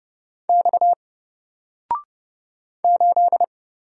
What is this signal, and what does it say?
Morse code, heard as a keyed tone, about 30 wpm, 700 Hz. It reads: X 8
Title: Single Letter-Number 30wpm